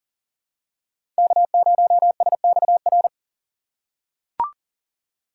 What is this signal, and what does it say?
Morse code, heard as a keyed tone, about 37 wpm, 700 Hz. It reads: K0SXR
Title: USA Call Signs - 5 Characters - 2x Character Spacing - 40wpm